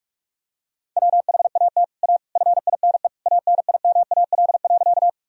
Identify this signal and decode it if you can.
Morse code, heard as a keyed tone, about 42 wpm, 700 Hz. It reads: WHAT A FINE ANIMAL.
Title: Sentences from Top 1000 Words - Review Part 1 - 45wpm